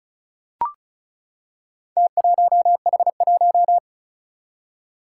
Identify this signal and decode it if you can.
Morse code, heard as a keyed tone, about 36 wpm, 700 Hz. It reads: T1H1
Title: Four Challenging Letter-Number Combinations 35wpm